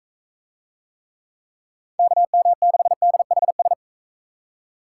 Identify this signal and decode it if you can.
Morse code, heard as a keyed tone, about 44 wpm, 700 Hz. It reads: KM6DHS